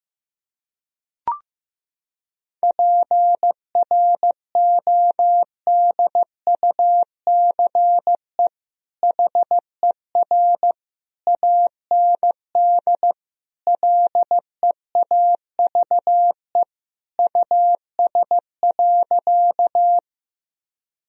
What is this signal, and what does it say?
Morse code, heard as a keyed tone, about 15 wpm, 700 Hz. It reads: PRODUCE HER AND LEAVE US.
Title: Sentences from Top 1000 Words - Review Part 3 - 15wpm